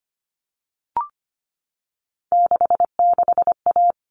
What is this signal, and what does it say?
Morse code, heard as a keyed tone, about 25 wpm, 700 Hz. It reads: 66A
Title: All Cut Numbers - 3-Digit 25wpm